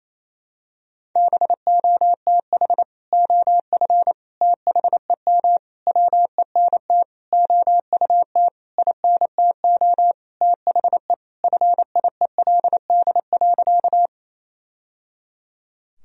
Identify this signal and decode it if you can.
Morse code, heard as a keyed tone, about 28 wpm, 700 Hz. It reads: BOTH OF THEM WENT OUT INTO THE FIELD.
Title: Sentences from Top 1000 Words - Review Part 2 - 28wpm